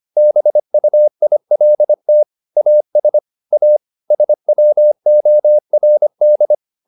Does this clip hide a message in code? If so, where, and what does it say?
Yes: Morse code from 0.2 to 6.5 s, heard as a keyed tone, about 25 wpm, 600 Hz. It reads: BUILT AS A SWORD